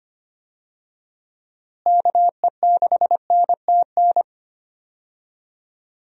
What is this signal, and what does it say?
Morse code, heard as a keyed tone, about 25 wpm, 700 Hz. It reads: KE6NTN
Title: USA Call Signs - 6 Characters - Repeated 3x - 25wpm